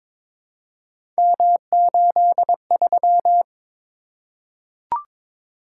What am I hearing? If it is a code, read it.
Morse code, heard as a keyed tone, about 22 wpm, 700 Hz. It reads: M83
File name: practice / 22wpm / Three Letter-Number Combinations - 22wpm